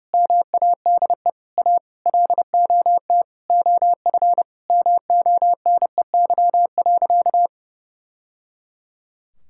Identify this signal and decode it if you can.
Morse code, heard as a keyed tone, about 30 wpm, 700 Hz. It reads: MADE A LOT OF MONEY.
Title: Sentences from Top 1000 Words - Review Part 6 - 30wpm